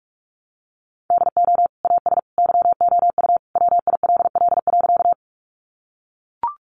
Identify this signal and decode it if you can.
Morse code, heard as a keyed tone, about 44 wpm, 700 Hz. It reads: DO AS YOU WILL.
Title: Sentences from Top 100 Words 45wpm